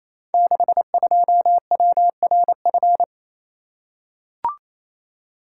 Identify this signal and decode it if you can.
Morse code, heard as a keyed tone, about 28 wpm, 700 Hz. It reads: 62WRF
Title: Code Groups - 5 Letter:Numbers - 28wpm